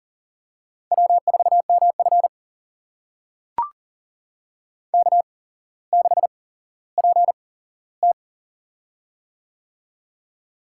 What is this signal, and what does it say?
Morse code, heard as a keyed tone, about 38 wpm, spaced as for 17 wpm, 700 Hz. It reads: W4MF K6PT
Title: USA Call Signs - 4 Characters - 8x Character Spacing - 40wpm